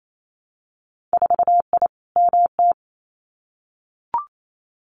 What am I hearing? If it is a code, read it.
Morse code, heard as a keyed tone, about 28 wpm, 700 Hz. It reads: 4I MT
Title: Winter Field Day Exchanges Encore 28wpm